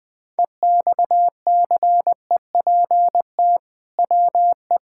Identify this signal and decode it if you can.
Morse code, heard as a keyed tone, about 20 wpm, 700 Hz. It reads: EXCEPT WE
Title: Sentences from Top 1600 Words 20wpm